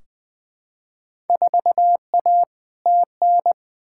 Morse code, heard as a keyed tone, about 20 wpm, 700 Hz. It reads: 4A TN
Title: Field Day Exchanges Encore - 20wpm